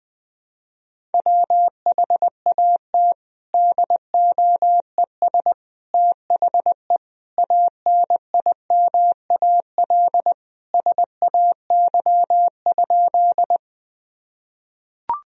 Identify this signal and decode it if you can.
Morse code, heard as a keyed tone, about 20 wpm, 700 Hz. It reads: WHAT DOES THE ANIMAL SAY?
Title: Sentences from Top 1000 Words - Review Part 1 - 20wpm